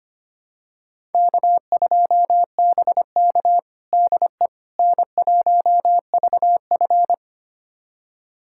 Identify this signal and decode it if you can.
Morse code, heard as a keyed tone, about 25 wpm, 700 Hz. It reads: K2BK DE N1VF
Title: USA Callsign DE Callsign - 4 Characters - 25wpm